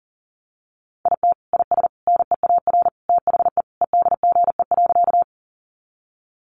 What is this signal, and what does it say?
Morse code, heard as a keyed tone, about 40 wpm, 700 Hz. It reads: IT IS NEAR THE EDGE.
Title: Additional Easy Phrases 40wpm